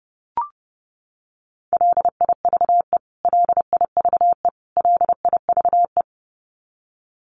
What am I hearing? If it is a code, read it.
Morse code, heard as a keyed tone, about 30 wpm, 700 Hz. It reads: LIVE LIVE LIVE
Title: Top 200 Words Repeated 30wpm